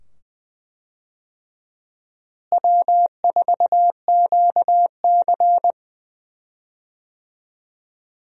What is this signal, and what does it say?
Morse code, heard as a keyed tone, about 21 wpm, 700 Hz. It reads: W4QC
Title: USA Call Signs - 4 Characters - Repeated 2x - 20wpm